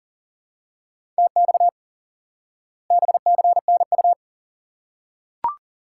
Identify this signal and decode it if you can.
Morse code, heard as a keyed tone, about 40 wpm, 700 Hz. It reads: TX BCNU
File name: IQE Course 3 - Lesson 89 - YAGI - Two Elements - 40wpm